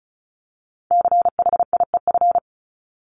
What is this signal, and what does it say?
Morse code, heard as a keyed tone, about 34 wpm, 700 Hz. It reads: CHIEF
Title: LY Suffix - Top 100 Words with Base Words 35wpm